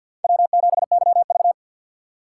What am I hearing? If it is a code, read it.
Morse code, heard as a keyed tone, about 53 wpm, 700 Hz. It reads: W7YV
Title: USA Call Signs - 4 Characters - 6x Character Spacing - 50wpm